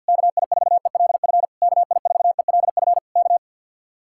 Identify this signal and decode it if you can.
Morse code, heard as a keyed tone, about 63 wpm, 700 Hz. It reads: KI4ELF KI4ELF K